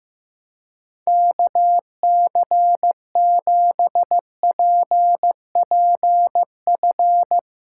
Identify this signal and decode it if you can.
Morse code, heard as a keyed tone, about 15 wpm, 700 Hz. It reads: KC7PPF
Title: USA Callsign DE Callsign - 6 Characters - 15wpm